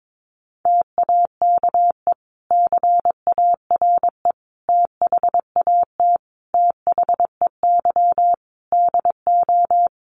Morse code, heard as a keyed tone, about 22 wpm, 700 Hz. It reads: TAKE CARE THAT THEY DO
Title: Sentences from Top 1500 Words 22wpm